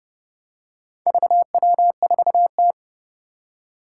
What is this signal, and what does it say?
Morse code, heard as a keyed tone, about 30 wpm, 700 Hz. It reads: VW4T
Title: Four Challenging Letter-Number Combinations 30wpm